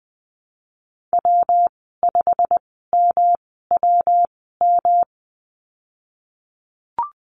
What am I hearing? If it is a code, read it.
Morse code, heard as a keyed tone, about 19 wpm, spaced as for 14 wpm, 700 Hz. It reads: W5MWM
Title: 122 Lesson 20g - Number 3 - Call Signs 20wpm